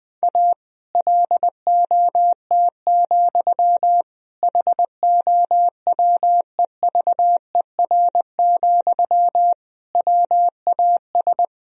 Morse code, heard as a keyed tone, about 20 wpm, 700 Hz. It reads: A LOT, HOWEVER, WAS
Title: Sentences from Top 1000 Words - Review Part 3 - 20wpm